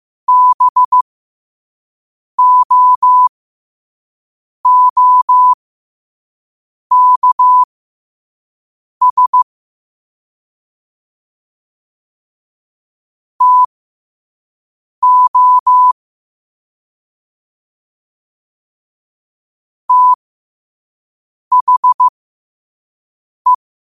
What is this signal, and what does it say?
Morse code, heard as a keyed tone, about 15 wpm, spaced as for 5 wpm, 1000 Hz. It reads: BOOKS TO THE
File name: qotdfiles / 5wpm